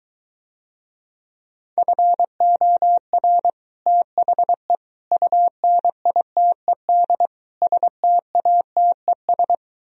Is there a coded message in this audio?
Yes: Morse code from 1.8 to 9.5 s, heard as a keyed tone, about 23 wpm, 700 Hz. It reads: FOR THE UNITED STATES